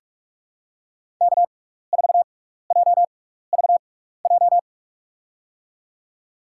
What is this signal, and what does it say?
Morse code, heard as a keyed tone, about 46 wpm, spaced as for 16 wpm, 700 Hz. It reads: K4JVJ
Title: USA Call Signs - 6x Character Spacing - 45wpm